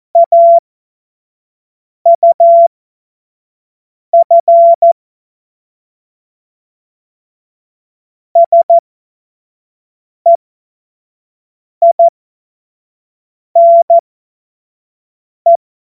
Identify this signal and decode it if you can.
Morse code, heard as a keyed tone, about 14 wpm, spaced as for 5 wpm, 680 Hz. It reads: AUF SEINE
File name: Morsekurs / Witze / Witz 19 Dezember 2023 25 F70